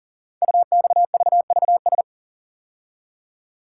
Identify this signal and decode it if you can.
Morse code, heard as a keyed tone, about 41 wpm, 700 Hz. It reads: UXVVS
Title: Code Groups - 5 Letters - 40wpm